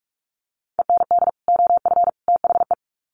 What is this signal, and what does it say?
Morse code, heard as a keyed tone, about 44 wpm, 700 Hz. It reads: END OF THE
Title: Sets of 3 Words 45wpm